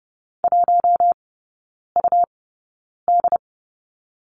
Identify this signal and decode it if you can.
Morse code, heard as a keyed tone, about 30 wpm, spaced as for 9 wpm, 700 Hz. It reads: 1UD
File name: Lesson 18e - Letter m - Three Characters 30wpm